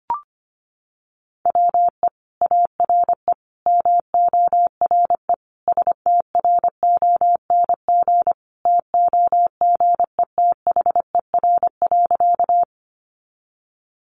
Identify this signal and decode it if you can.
Morse code, heard as a keyed tone, about 25 wpm, 700 Hz. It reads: WE ARE MORE STRONG TOGETHER.
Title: Sentences from Top 500 Words 25wpm